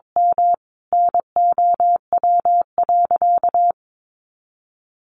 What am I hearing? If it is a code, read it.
Morse code, heard as a keyed tone, about 22 wpm, 700 Hz. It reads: M NOW.
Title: Sentences from Top 1100 Words 22wpm